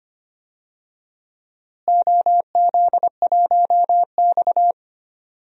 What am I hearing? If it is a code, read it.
Morse code, heard as a keyed tone, about 25 wpm, 700 Hz. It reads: OZ1X
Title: DXCC-100 Call Signs - 4 Characters - Repeated 3x - 25wpm